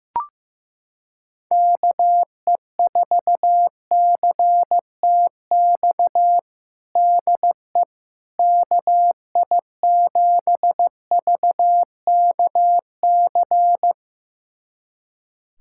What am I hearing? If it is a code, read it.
Morse code, heard as a keyed tone, about 15 wpm, 700 Hz. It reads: KE4CTX DE KI7VKC
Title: USA Callsign DE Callsign - 6 Characters - 15wpm